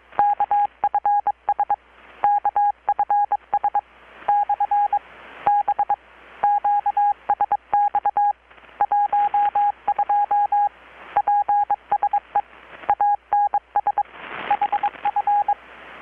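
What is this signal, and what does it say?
Morse code, heard as a keyed tone, about 22 wpm, 820 Hz. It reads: KFS KFS/B QSX 12 PSE ANS HF
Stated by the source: Morse from KPH's Night of Nights XX maritime radiotelegraphy event, received in Chiapas, Mexico about 3,800 km away on a Kenwood R-600 with a grounded 10 m long-wire antenna.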